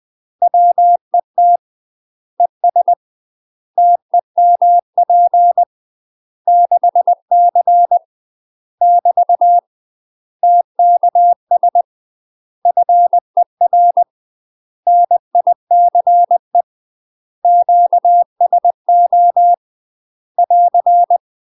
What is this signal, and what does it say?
Morse code, heard as a keyed tone, about 20 wpm, 700 Hz. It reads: WET ES TEMP 6C <BT> TKS FER NICE QSO <AR>
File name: UK MASTER QSOs TRAINING / 20WPM / UK MASTER QSOs 0022